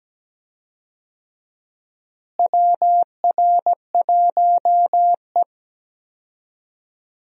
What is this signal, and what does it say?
Morse code, heard as a keyed tone, about 17 wpm, 700 Hz. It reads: WR1E
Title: USA Call Signs - 4 Characters - Repeated 3x - 17wpm